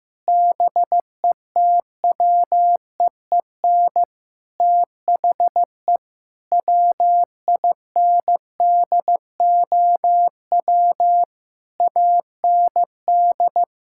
Morse code, heard as a keyed tone, about 15 wpm, 700 Hz. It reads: BETWEEN THE WINDOW AND
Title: Sentences from Top 1600 Words 15wpm